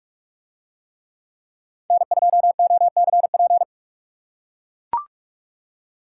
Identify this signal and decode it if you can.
Morse code, heard as a keyed tone, about 43 wpm, 700 Hz. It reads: N2OCP